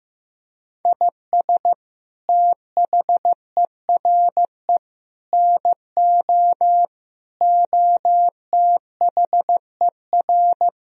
Morse code, heard as a keyed tone, about 15 wpm, 700 Hz. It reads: IS THERE NO OTHER
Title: Sentences from Top 1000 Words - Review Part 2 - 15wpm